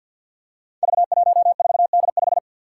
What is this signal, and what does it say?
Morse code, heard as a keyed tone, about 48 wpm, 700 Hz. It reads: V14D5
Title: Five Challenging Letter-Number Combinations 50wpm